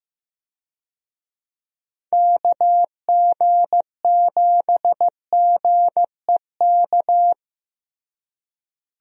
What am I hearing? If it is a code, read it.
Morse code, heard as a keyed tone, about 15 wpm, 700 Hz. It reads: KG7GEK
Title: USA Call Signs - 6 Characters - Repeated 2x - 15wpm